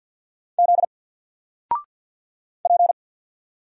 Morse code, heard as a keyed tone, about 50 wpm, 700 Hz. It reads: Z P
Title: Single Letter-Number 50wpm